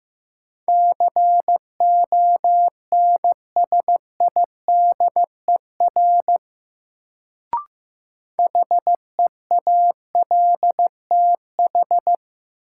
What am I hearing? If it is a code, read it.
Morse code, heard as a keyed tone, about 15 wpm, 700 Hz. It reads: CONSIDER HEALTH
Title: Sentences from Top 600 Words - New Word Drill - 15wpm